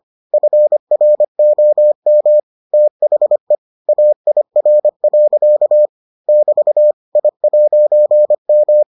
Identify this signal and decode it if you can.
Morse code, heard as a keyed tone, about 25 wpm, 600 Hz. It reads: FROM THE AIR. <BT> I'M